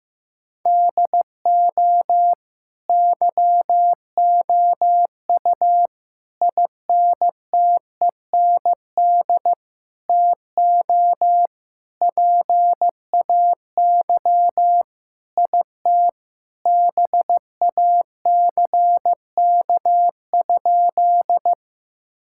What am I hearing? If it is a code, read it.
Morse code, heard as a keyed tone, about 16 wpm, 700 Hz. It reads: DO YOU INTEND TO PAY IT BACK?
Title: Sentences from Top 1600 Words 15wpm